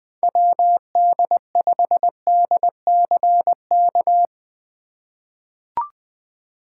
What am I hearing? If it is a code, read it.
Morse code, heard as a keyed tone, about 20 wpm, 700 Hz. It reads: WD5DCK
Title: USA Call Signs - 6 Characters - 4x Character Spacing - 20wpm